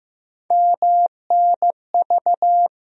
Morse code, heard as a keyed tone, about 15 wpm, 700 Hz. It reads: MNV
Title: USA Call Sign Postfixes - 3 Letters 15wpm